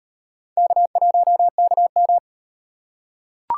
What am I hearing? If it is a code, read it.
Morse code, heard as a keyed tone, about 38 wpm, 700 Hz. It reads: K1KM